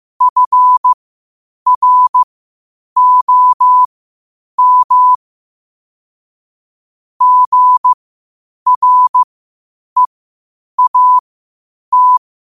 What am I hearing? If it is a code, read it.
Morse code, heard as a keyed tone, about 15 wpm, spaced as for 8 wpm, 1000 Hz. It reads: FROM GREAT
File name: 8wpm